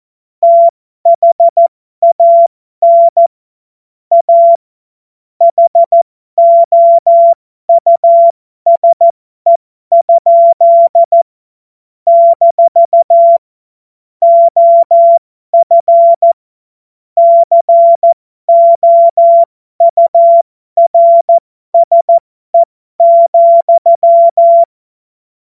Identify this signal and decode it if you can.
Morse code, heard as a keyed tone, about 14 wpm, spaced as for 12 wpm, 680 Hz. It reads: THAN A HOUSE? - OF COURSE,
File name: Morsekurs / Witze